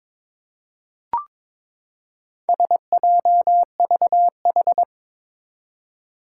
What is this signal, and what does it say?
Morse code, heard as a keyed tone, about 22 wpm, 700 Hz. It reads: SJVH
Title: Four Challenging Letter-Number Combinations 22wpm